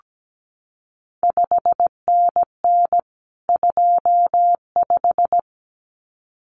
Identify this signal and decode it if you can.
Morse code, heard as a keyed tone, about 17 wpm, 700 Hz. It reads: 5NN 25
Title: CQ WW DX Contest Exchange - 17wpm